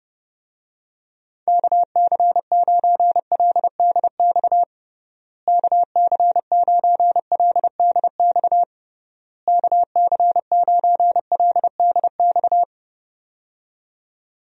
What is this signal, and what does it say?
Morse code, heard as a keyed tone, about 30 wpm, 700 Hz. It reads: KC9LDX KC9LDX KC9LDX
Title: USA Call Signs - 6 Characters - Repeated 3x - 30wpm